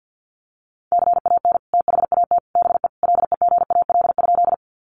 Morse code, heard as a keyed tone, about 50 wpm, 700 Hz. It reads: CAN THAT BE LEGAL?
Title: Sentences from Top 1200 Words 50wpm